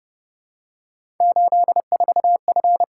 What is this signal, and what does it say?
Morse code, heard as a keyed tone, about 30 wpm, 700 Hz. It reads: 84F
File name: Speed-Racing - Three Letter-Number Combinations - 25wpm